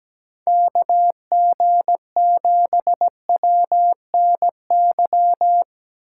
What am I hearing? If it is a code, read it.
Morse code, heard as a keyed tone, about 17 wpm, 700 Hz. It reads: KG7WNY
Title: USA Call Signs - 6 Characters - 2x Character Spacing - 17wpm